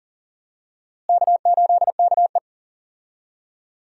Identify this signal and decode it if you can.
Morse code, heard as a keyed tone, about 40 wpm, 700 Hz. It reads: K8KE
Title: USA Call Signs - 4 Characters - Repeated 3x - 40wpm